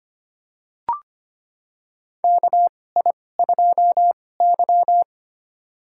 Morse code, heard as a keyed tone, about 25 wpm, spaced as for 18 wpm, 700 Hz. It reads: KI2Y
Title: USA Call Signs - 4 Characters - 2x Character Spacing - 25wpm